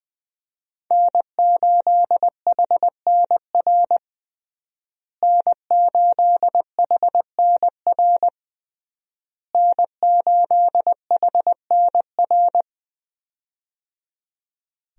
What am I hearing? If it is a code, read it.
Morse code, heard as a keyed tone, about 20 wpm, 700 Hz. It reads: N8HNR N8HNR N8HNR
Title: USA Call Signs - 5 Characters - Repeated 3x - 20wpm